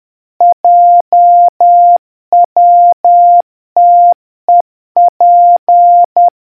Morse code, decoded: JWTEP